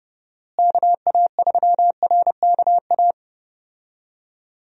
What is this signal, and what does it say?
Morse code, heard as a keyed tone, about 30 wpm, 700 Hz. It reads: KA3RKA
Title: USA Call Signs - 6 Characters - Repeated 2x - 30wpm